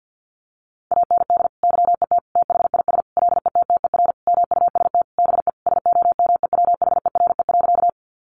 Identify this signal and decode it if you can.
Morse code, heard as a keyed tone, about 50 wpm, 700 Hz. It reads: AND YET THIS LETTER MUST BE SOMEWHERE.
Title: Sentences from Top 1400 Words 50wpm